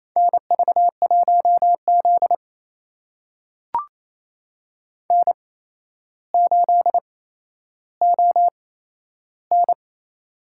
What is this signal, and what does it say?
Morse code, heard as a keyed tone, about 28 wpm, spaced as for 12 wpm, 700 Hz. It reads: NV1Z N8ON
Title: USA Call Signs - 4 Characters - 8x Character Spacing - 28wpm